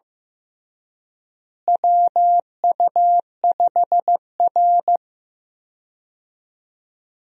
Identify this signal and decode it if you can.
Morse code, heard as a keyed tone, about 15 wpm, 700 Hz. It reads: WU5R
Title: USA Call Signs - 4 Characters - Repeated 3x - 15wpm